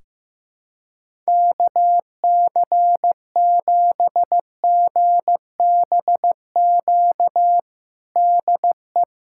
Morse code, heard as a keyed tone, about 15 wpm, 700 Hz. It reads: KC7GBQ DE
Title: USA Callsign DE Callsign - 6 Characters - 15wpm